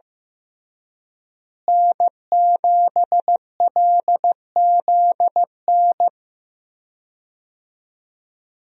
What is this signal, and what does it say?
Morse code, heard as a keyed tone, about 15 wpm, 700 Hz. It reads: N7LZN
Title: USA Call Signs - 5 Characters - Repeated 2x - 15wpm